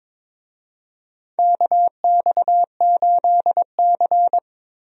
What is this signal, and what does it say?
Morse code, heard as a keyed tone, about 22 wpm, 700 Hz. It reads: KX8C
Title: USA Call Signs - 4 Characters - Repeated 2x - 22wpm